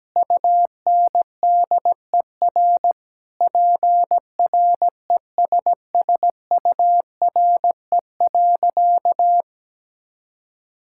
Morse code, heard as a keyed tone, about 17 wpm, 700 Hz. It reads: UNDER PRESSURE.
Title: Sentences from Top 1000 Words - Review Part 6 - 17wpm